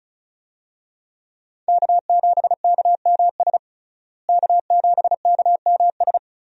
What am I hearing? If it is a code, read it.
Morse code, heard as a keyed tone, about 36 wpm, 700 Hz. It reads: K7KMS K7KMS
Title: USA Call Signs - 5 Characters - Repeated 3x - 35wpm